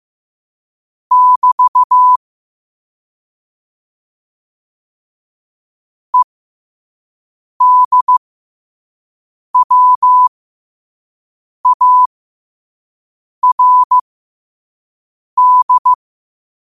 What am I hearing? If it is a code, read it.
Morse code, heard as a keyed tone, about 15 wpm, spaced as for 5 wpm, 1000 Hz. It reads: <BT> EDWARD
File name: qotdfiles / 5wpm